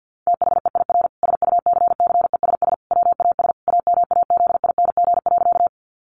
Morse code, heard as a keyed tone, about 50 wpm, 700 Hz. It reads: THEIR SUCCESS WAS AMAZING.
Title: Sentences from Top 1500 Words 50wpm